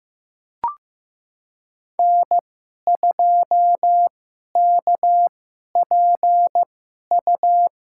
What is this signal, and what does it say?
Morse code, heard as a keyed tone, about 15 wpm, spaced as for 11 wpm, 700 Hz. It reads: N2KPU
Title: USA Call Signs - 5 Characters - 2x Character Spacing - 15wpm